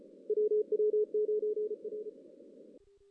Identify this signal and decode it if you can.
Morse code, heard as a keyed tone, about 33 wpm, 470 Hz. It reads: WW9R
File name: 2016 JAN CW NAQP / 2016 Jan naqp k5ot - 0-6